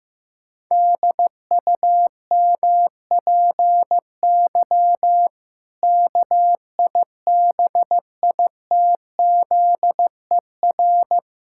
Morse code, heard as a keyed tone, about 15 wpm, 700 Hz. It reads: DUMPY KIBITZER
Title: Pangram Challenge v2 - 15wpm